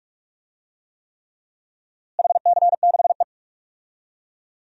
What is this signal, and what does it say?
Morse code, heard as a keyed tone, about 46 wpm, 700 Hz. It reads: HC6E
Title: DXCC-100 Call Signs - 4 Characters - Repeated 3x - 45wpm